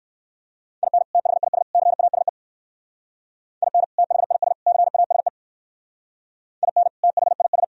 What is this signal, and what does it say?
Morse code, heard as a keyed tone, about 56 wpm, 700 Hz. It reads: IN THIS CASE IN THIS CASE IN THIS
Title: Speed-Racing - Sets of 3 Words 45wpm